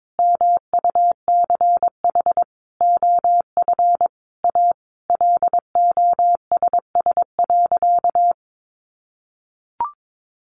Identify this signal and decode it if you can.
Morse code, heard as a keyed tone, about 22 wpm, 700 Hz. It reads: MUCH OF A LOSS.
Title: Sentences from Top 1400 Words 22wpm